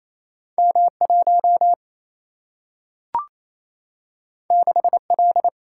Morse code, heard as a keyed tone, about 28 wpm, 700 Hz. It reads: M 1 6 L
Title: Single-Letter Single-Number 28wpm